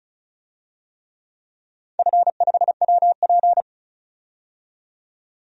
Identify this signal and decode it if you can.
Morse code, heard as a keyed tone, about 34 wpm, 700 Hz. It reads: F5WP
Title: DXCC-100 Call Signs - 4 Characters - 35wpm